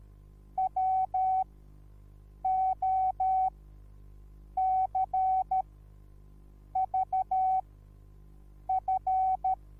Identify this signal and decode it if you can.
Morse code, heard as a keyed tone, about 13 wpm, spaced as for 6 wpm, 760 Hz. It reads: WOCVF